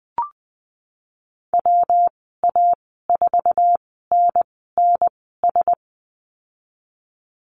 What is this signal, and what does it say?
Morse code, heard as a keyed tone, about 19 wpm, spaced as for 14 wpm, 700 Hz. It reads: WA4NNS
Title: 122 Lesson 20g - Number 3 - Call Signs 20wpm